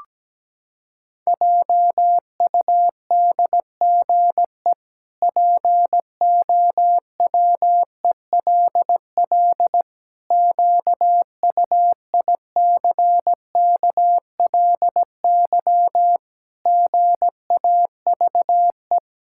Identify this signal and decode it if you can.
Morse code, heard as a keyed tone, about 17 wpm, 700 Hz. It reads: JUDGE POWELL QUICKLY GAVE
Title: Pangram Challenge v2 - 17wpm